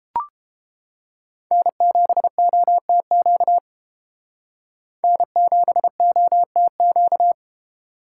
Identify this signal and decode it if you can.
Morse code, heard as a keyed tone, about 30 wpm, 700 Hz. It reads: N7OTQ N7OTQ